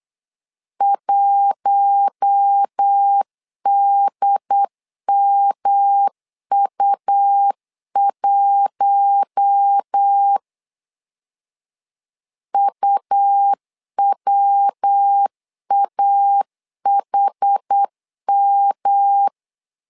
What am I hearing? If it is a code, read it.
Morse code, heard as a keyed tone, about 8 wpm, 800 Hz. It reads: 1DMU1 UWAHM
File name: LEZIONI / 4 Q1 8wpm 3035